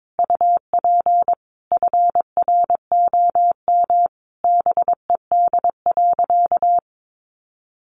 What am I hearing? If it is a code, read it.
Morse code, heard as a keyed tone, about 22 wpm, 700 Hz. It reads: UP FROM BED.
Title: Sentences from Top 900 Words 22wpm